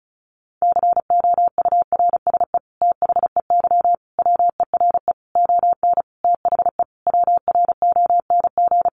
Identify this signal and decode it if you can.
Morse code, heard as a keyed tone, about 34 wpm, 700 Hz. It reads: COURSE THEY WERE ON THE WRONG
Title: Sentences from Top 900 Words 35wpm